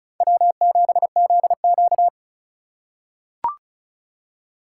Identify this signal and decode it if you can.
Morse code, heard as a keyed tone, about 34 wpm, 700 Hz. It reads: W7ZQ
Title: USA Call Signs - 4 Characters - 3x Character Spacing - 35wpm